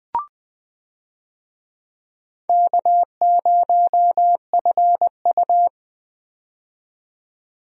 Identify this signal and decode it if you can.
Morse code, heard as a keyed tone, about 21 wpm, 700 Hz. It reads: K0FU
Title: USA Call Signs - 4 Characters - 20wpm